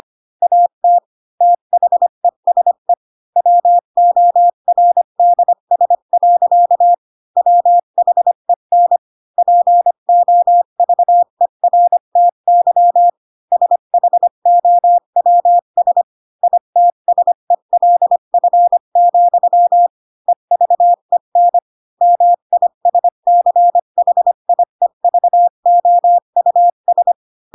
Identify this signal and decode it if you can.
Morse code, heard as a keyed tone, about 25 wpm, 700 Hz. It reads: AT THESE WORDS. WHEN POVERTY SHOWS ITSELF, EVEN MISCHIEVOUS
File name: Pinocchio / Pinocchio ch8 20 25 ik1hge